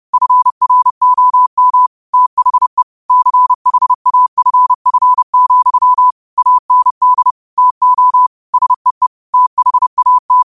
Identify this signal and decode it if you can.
Morse code, heard as a keyed tone, about 30 wpm, 1000 Hz. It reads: FROM THE CHAFF, AND TO SEE THAT
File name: qotdfiles / 30wpm